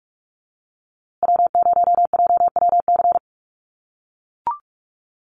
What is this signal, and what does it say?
Morse code, heard as a keyed tone, about 44 wpm, 700 Hz. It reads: W0JWC
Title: USA Call Signs Repeated 2x 45wpm